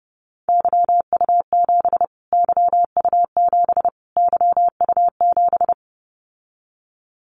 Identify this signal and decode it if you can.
Morse code, heard as a keyed tone, about 30 wpm, 700 Hz. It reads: YU7 YU7 YU7
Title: Callsign Prefixes for Top 25 DXCC - Repeated 3x - 30wpm